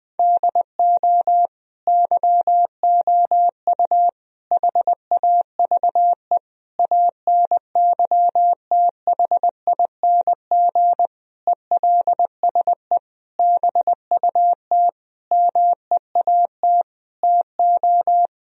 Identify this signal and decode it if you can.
Morse code, heard as a keyed tone, about 20 wpm, 700 Hz. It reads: DO YOU HAVE ANYTHING ELSE BUT MEAT TO
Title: Sentences from Top 1000 Words - Review Part 3 - 20wpm